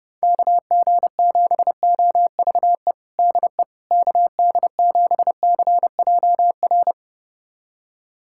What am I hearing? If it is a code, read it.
Morse code, heard as a keyed tone, about 32 wpm, 700 Hz. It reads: KG7OVE DE KD7CJR
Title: USA Callsign DE Callsign 30wpm